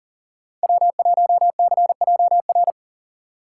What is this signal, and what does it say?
Morse code, heard as a keyed tone, about 40 wpm, 700 Hz. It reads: W1CJR
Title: USA Call Signs - 5 Characters - 4x Character Spacing - 40wpm